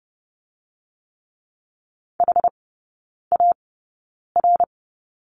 Morse code, heard as a keyed tone, about 30 wpm, spaced as for 9 wpm, 700 Hz. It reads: HAR
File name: 081 Lesson 14e - Number 2 - Three Characters 30wpm